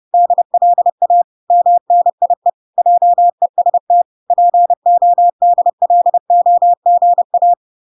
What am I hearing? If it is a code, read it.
Morse code, heard as a keyed tone, about 30 wpm, 700 Hz. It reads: DLA MNIE JEST PODLOGA